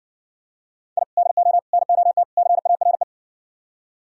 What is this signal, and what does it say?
Morse code, heard as a keyed tone, about 57 wpm, 700 Hz. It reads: I DO NOT CARE